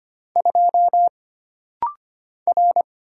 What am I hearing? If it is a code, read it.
Morse code, heard as a keyed tone, about 25 wpm, 700 Hz. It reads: 2 R
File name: Single Character (Letter:Number:Punctuation:Prosign) - Rapid-Fire - 25wpm